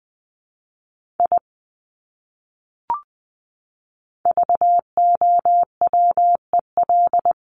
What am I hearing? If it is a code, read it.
Morse code, heard as a keyed tone, about 20 wpm, 700 Hz. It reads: I VOWEL
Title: Top 500 Words Repeated 20wpm